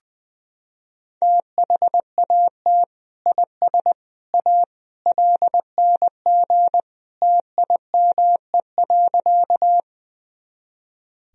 Morse code, heard as a keyed tone, about 20 wpm, 700 Hz. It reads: THAT IS A LNG TIME.
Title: Sentences from Top 100 Words and Abbreviations 20wpm